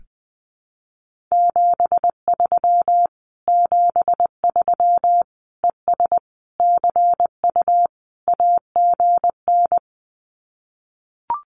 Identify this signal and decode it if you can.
Morse code, heard as a keyed tone, about 20 wpm, 700 Hz. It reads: 73 73 ES CU AGN
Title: Common QSO Phrases Encore 20wpm